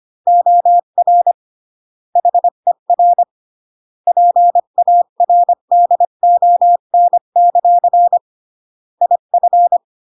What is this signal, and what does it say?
Morse code, heard as a keyed tone, about 25 wpm, 700 Hz. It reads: OR HER PARDON; IF